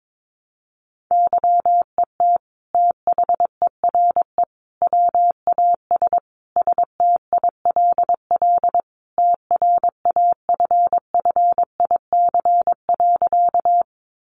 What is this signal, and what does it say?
Morse code, heard as a keyed tone, about 22 wpm, 700 Hz. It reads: YET THERE WAS STILL TRAFFIC.
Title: Sentences from Top 1300 Words 22wpm